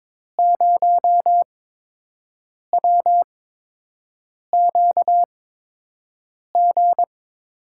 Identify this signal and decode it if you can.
Morse code, heard as a keyed tone, about 22 wpm, spaced as for 6 wpm, 700 Hz. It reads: 0WQG
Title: USA Call Signs - 6 Characters - 8x Character Spacing - 22wpm